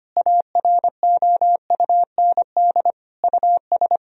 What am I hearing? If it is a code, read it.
Morse code, heard as a keyed tone, about 25 wpm, 700 Hz. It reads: AROUND US
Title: Sentences from Top 1000 Words - Review Part 3 - 25wpm